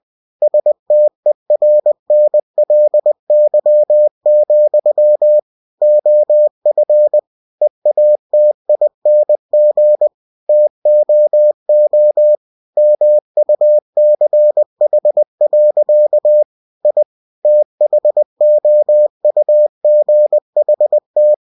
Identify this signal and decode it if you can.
Morse code, heard as a keyed tone, about 20 wpm, 600 Hz. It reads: STERNLY, OF EATING TOO MUCH. I THOUGHT
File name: cw-books / winnie-the-pooh / winnie-20wpm-page-0017